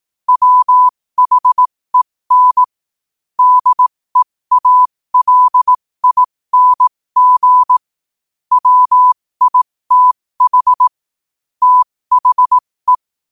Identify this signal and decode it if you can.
Morse code, heard as a keyed tone, about 18 wpm, spaced as for 15 wpm, 1000 Hz. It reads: WHEN DEALING WITH THE
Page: New quotes every day in morse code at 15 Words per minute.